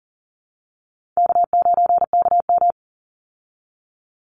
Morse code, heard as a keyed tone, about 40 wpm, 700 Hz. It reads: K9KM
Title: Call Signs Repeated 3x - 40wpm